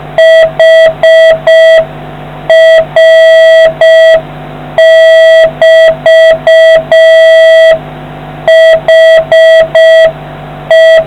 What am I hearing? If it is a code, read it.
Morse code, heard as a keyed tone, about 6 wpm, 650 Hz, with noise. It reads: HR<BT>HE